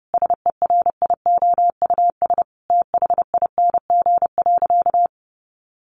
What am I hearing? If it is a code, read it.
Morse code, heard as a keyed tone, about 30 wpm, 700 Hz. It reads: SERIOUS THING.
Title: Sentences from Top 1500 Words 30wpm